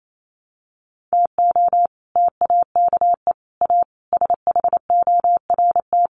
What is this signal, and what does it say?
Morse code, heard as a keyed tone, about 28 wpm, 700 Hz. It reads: TO TAKE A SHORT
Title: Sets of 5 Words Encore2 28wpm